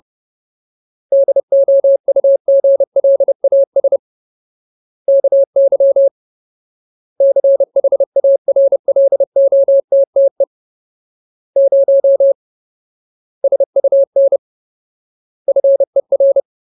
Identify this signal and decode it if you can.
Morse code, heard as a keyed tone, about 30 wpm, 550 Hz. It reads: DOUGLAS KY CHARLOTTE 0 SUN FER
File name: Ultimate Word List / 30WPM Ultimate